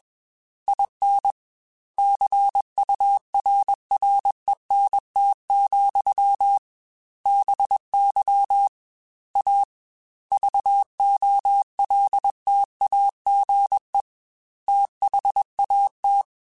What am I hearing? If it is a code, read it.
Morse code, heard as a keyed tone, about 21 wpm, 800 Hz. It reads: IN CURRENT, BY A VOLTAGE THAT